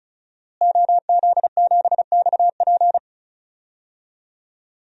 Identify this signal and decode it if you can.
Morse code, heard as a keyed tone, about 36 wpm, 700 Hz. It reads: OZ7XP
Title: DXCC-100 Call Signs - 5 Characters - Repeated 2x - 35wpm